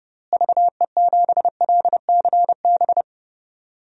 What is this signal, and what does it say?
Morse code, heard as a keyed tone, about 32 wpm, 700 Hz. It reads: VE7LCB
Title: DXCC-100 Call Signs - 6 Characters - Repeated 3x - 30wpm